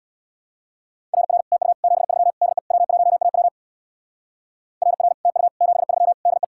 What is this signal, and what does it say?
Morse code, heard as a keyed tone, about 64 wpm, 700 Hz. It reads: RR TU 73 DE W1AW RR TU 73 DE